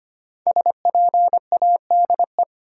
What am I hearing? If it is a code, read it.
Morse code, heard as a keyed tone, about 25 wpm, 690 Hz. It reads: SPADE
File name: practice / 25wpm / Easy Words with Hints - 25wpm